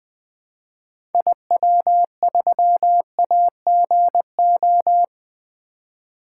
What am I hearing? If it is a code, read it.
Morse code, heard as a keyed tone, about 20 wpm, 700 Hz. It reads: IW3AGO
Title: 2023 SOTA Activator Call Signs - 20wpm